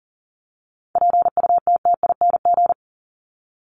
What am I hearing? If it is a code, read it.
Morse code, heard as a keyed tone, about 40 wpm, 700 Hz. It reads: PUTTING
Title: ING Suffix - Top 100 Words with Base Words 40wpm